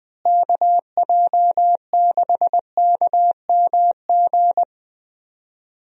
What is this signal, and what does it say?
Morse code, heard as a keyed tone, about 20 wpm, 700 Hz. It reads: KJ6KMG
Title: USA Call Signs - 6 Characters - Repeated 2x - 20wpm